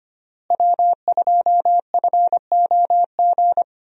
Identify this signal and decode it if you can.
Morse code, heard as a keyed tone, about 25 wpm, 700 Hz. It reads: W2FOG
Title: USA Call Signs - 5 Characters - Repeated 3x - 25wpm